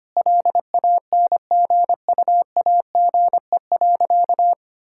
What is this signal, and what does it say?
Morse code, heard as a keyed tone, about 25 wpm, 700 Hz. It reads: LANGUAGE.
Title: Sentences from Top 1600 Words 25wpm